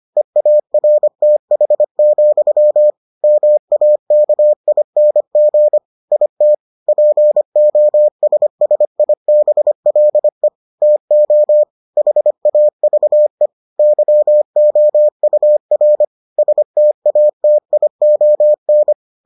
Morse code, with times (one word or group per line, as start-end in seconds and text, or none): 0.2-2.9 EARTH,
3.2-5.8 MAKING
6.1-6.5 IT
6.9-10.5 POSSIBLE
10.8-11.6 TO
12.0-13.5 HAVE
13.8-16.0 YOUR
16.4-18.9 STATION